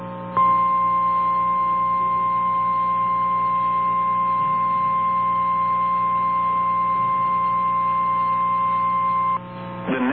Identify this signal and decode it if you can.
Emergency Alert System (SAME) tones (the 1050 Hz attention tone), with noise. It is heard from 0.3 to 9.4 s.
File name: eas / KNCYKBIE / KNCYKBIE